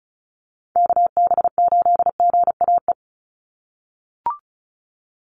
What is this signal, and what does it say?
Morse code, heard as a keyed tone, about 34 wpm, 700 Hz. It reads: KB8GAE
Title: Call Signs - 35wpm